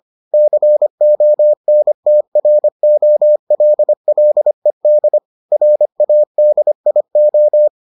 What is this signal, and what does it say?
Morse code, heard as a keyed tone, about 25 wpm, 600 Hz. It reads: CONTROLLED RADIO